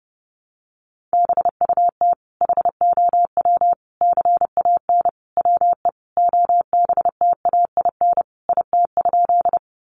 Morse code, heard as a keyed tone, about 30 wpm, 700 Hz. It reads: BUT HOW CAN WE OBTAIN IT?
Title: Sentences from Top 1300 Words 30wpm